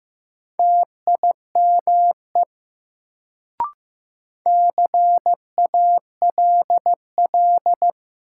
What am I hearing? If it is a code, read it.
Morse code, heard as a keyed tone, about 15 wpm, 700 Hz. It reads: TIME CALL
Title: Sentences from Top 100 Words - New Word Drill - 15wpm